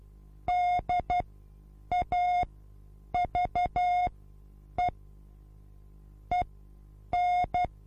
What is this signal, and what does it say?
Morse code, heard as a keyed tone, about 12 wpm, spaced as for 8 wpm, 720 Hz. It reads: DAVE EN
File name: repos / Morsekurs / DJ4UF Morsen24